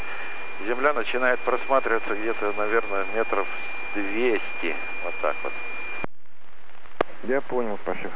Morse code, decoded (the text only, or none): none